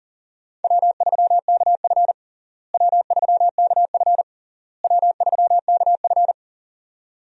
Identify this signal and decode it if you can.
Morse code, heard as a keyed tone, about 40 wpm, 700 Hz. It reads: W3KF W3KF W3KF
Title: USA Call Signs - 4 Characters - Repeated 3x - 40wpm